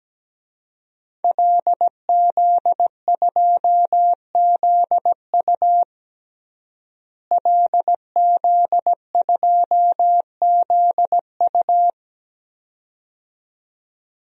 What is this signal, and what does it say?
Morse code, heard as a keyed tone, about 17 wpm, 700 Hz. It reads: LZ2ZU LZ2ZU
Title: DXCC-100 Call Signs - 5 Characters - Repeated 2x - 17wpm